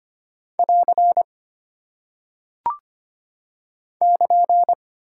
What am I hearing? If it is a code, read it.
Morse code, heard as a keyed tone, about 25 wpm, 700 Hz. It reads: <AR> <KN>
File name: Single Punctuation-Prosign - 25wpm